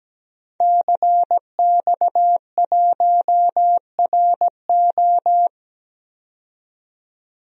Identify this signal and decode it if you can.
Morse code, heard as a keyed tone, about 17 wpm, 700 Hz. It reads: CX1RO
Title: DXCC-100 Call Signs - 5 Characters - 17wpm